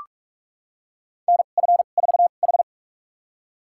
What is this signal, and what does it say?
Morse code, heard as a keyed tone, about 46 wpm, spaced as for 32 wpm, 700 Hz. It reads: NF4H
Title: USA Call Signs - 4 Characters - 2x Character Spacing - 45wpm